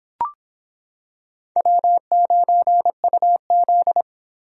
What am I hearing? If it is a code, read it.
Morse code, heard as a keyed tone, about 26 wpm, 700 Hz. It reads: W9UZ